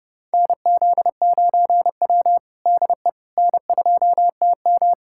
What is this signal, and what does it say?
Morse code, heard as a keyed tone, about 30 wpm, 700 Hz. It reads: NZ9W DE N2TM
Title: USA Callsign DE Callsign - 4 Characters - 30wpm